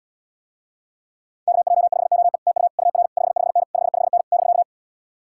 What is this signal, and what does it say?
Morse code, heard as a keyed tone, about 78 wpm, 700 Hz. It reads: KJ5OE TU UR 55N 55N <BK>